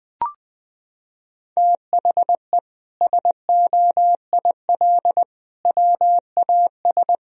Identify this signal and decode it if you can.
Morse code, heard as a keyed tone, about 20 wpm, 700 Hz. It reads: THE SOIL WAS
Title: Sentences from Top 1000 Words - Review Part 4 - 20wpm